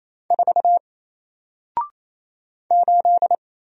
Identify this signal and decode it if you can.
Morse code, heard as a keyed tone, about 27 wpm, 700 Hz. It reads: 4 8
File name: Single-Letter-Number-28wpm